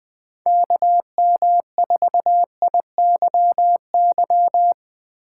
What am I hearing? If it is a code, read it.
Morse code, heard as a keyed tone, about 20 wpm, 700 Hz. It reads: KM4IYY
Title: Speed Racing - USA Call Signs 20wpm